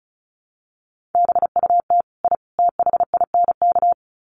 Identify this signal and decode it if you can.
Morse code, heard as a keyed tone, about 34 wpm, 700 Hz. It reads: BUT I THINK
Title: Sets of 3 Words 35wpm